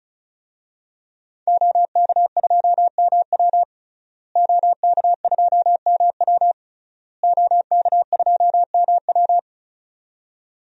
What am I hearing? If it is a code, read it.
Morse code, heard as a keyed tone, about 36 wpm, 700 Hz. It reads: OK2MW OK2MW OK2MW
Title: DXCC-100 Call Signs - 5 Characters - Repeated 3x - 35wpm